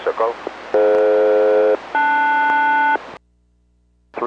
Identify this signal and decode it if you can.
Aircraft SELCAL, heard as two pairs of tones, with speech. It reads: DF-LS